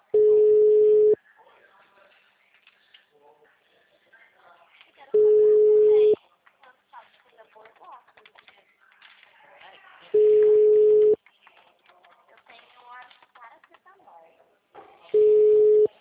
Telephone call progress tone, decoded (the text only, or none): ringback tone